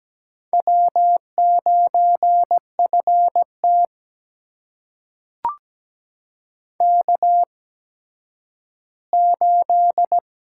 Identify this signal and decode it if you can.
Morse code, heard as a keyed tone, about 17 wpm, spaced as for 10 wpm, 700 Hz. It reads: W9FT K8
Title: USA Call Signs - 4 Characters - 8x Character Spacing - 17wpm